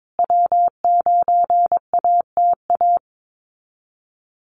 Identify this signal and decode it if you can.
Morse code, heard as a keyed tone, about 22 wpm, 700 Hz. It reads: W9ATA
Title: USA Call Signs Repeated 3x 22wpm